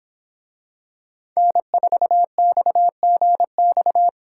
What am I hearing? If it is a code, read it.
Morse code, heard as a keyed tone, about 24 wpm, 700 Hz. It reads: N4XGX